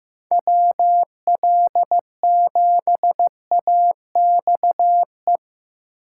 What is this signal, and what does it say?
Morse code, heard as a keyed tone, about 18 wpm, 700 Hz. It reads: WL7AXE
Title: Speed Racing - USA 2x3 Call Signs 15wpm